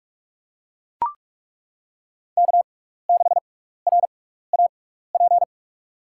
Morse code, heard as a keyed tone, about 42 wpm, 700 Hz. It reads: K 6 R A P
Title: USA Call Signs - 5 Characters - 6x Character Spacing - 45wpm